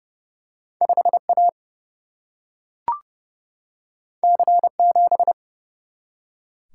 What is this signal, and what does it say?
Morse code, heard as a keyed tone, about 30 wpm, 700 Hz. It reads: 5A C7
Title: Single-Letter Single-Number 30wpm